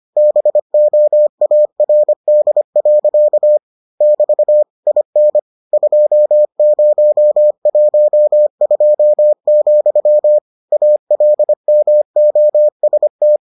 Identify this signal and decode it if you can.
Morse code, heard as a keyed tone, about 25 wpm, 600 Hz. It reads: BOARD. <BT> IN 2012, ALMOST